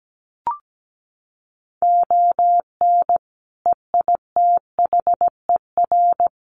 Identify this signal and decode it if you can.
Morse code, heard as a keyed tone, about 17 wpm, 700 Hz. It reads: ON EITHER
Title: Sets of 5 Words 17wpm